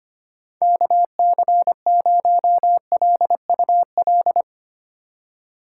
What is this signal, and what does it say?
Morse code, heard as a keyed tone, about 25 wpm, 700 Hz. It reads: KC0LUL
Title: USA Call Signs - 6 Characters - 2x Character Spacing - 25wpm